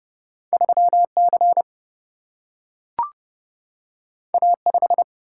Morse code, heard as a keyed tone, about 30 wpm, 700 Hz. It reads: 3 C A 5
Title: Single-Letter Single-Number 30wpm